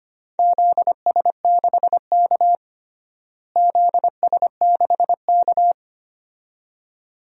Morse code, heard as a keyed tone, about 25 wpm, 700 Hz. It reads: ZS6K ZS6K
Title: DXCC-100 Call Signs - 4 Characters - Repeated 2x - 25wpm